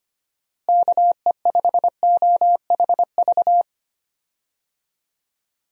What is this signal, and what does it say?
Morse code, heard as a keyed tone, about 25 wpm, 700 Hz. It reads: KE5OHV
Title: USA Call Signs - 6 Characters - Repeated 3x - 25wpm